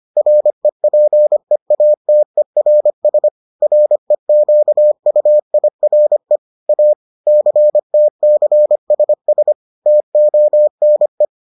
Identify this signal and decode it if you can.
Morse code, heard as a keyed tone, about 25 wpm, 600 Hz. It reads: REPEATERS REQUIRE A CTCSS TONE